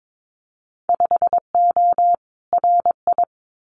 Morse code, heard as a keyed tone, about 22 wpm, 700 Hz. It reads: 5O RI
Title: Winter Field Day Exchanges Encore 22wpm